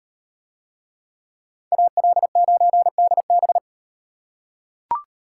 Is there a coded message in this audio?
Yes: Morse code from 1.7 to 3.6 s, heard as a keyed tone, about 38 wpm, 700 Hz. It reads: AL9DB